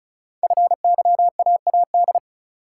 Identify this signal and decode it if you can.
Morse code, heard as a keyed tone, about 36 wpm, 690 Hz. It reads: FYAAD
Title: Code Groups - 5 Letters - 35wpm